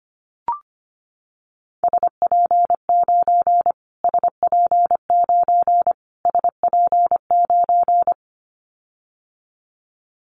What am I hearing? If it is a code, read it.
Morse code, heard as a keyed tone, about 25 wpm, 700 Hz. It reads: SP9 SP9 SP9
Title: Callsign Prefixes for Top 25 DXCC - Repeated 3x - 25wpm